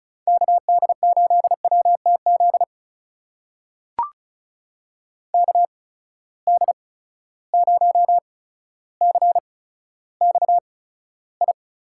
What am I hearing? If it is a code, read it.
Morse code, heard as a keyed tone, about 36 wpm, spaced as for 15 wpm, 700 Hz. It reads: KD8WTZ KD0CXI
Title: USA Call Signs - 6 Characters - 8x Character Spacing - 35wpm